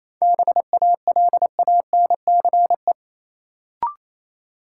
Morse code, heard as a keyed tone, about 28 wpm, 700 Hz. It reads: BALANCE
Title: Sentences from Top 900 Words - New Word Drill - 28wpm